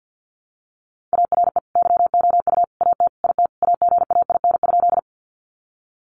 Morse code, heard as a keyed tone, about 50 wpm, 700 Hz. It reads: ARE YOU AT IT AGAIN?
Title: Sentences from Top 200 Words 50wpm